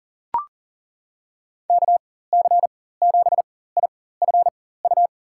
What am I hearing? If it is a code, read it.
Morse code, heard as a keyed tone, about 40 wpm, spaced as for 19 wpm, 700 Hz. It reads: KC7IFU
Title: USA Call Signs - 6 Characters - 4x Character Spacing - 40wpm